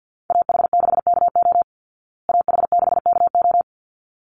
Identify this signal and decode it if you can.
Morse code, heard as a keyed tone, about 49 wpm, 700 Hz. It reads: AH6KO AH6KO
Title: Call Signs Repeated 3x - 50wpm